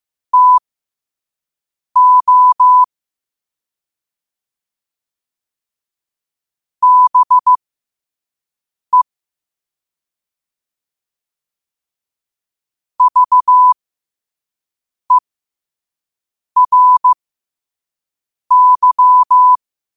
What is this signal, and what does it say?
Morse code, heard as a keyed tone, about 15 wpm, spaced as for 5 wpm, 1000 Hz. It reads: TO BE VERY
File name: qotdfiles / 5wpm